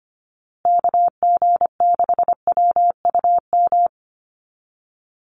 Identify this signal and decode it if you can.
Morse code, heard as a keyed tone, about 25 wpm, 700 Hz. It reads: KG6WUM
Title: USA Call Signs Repeated 3x 25wpm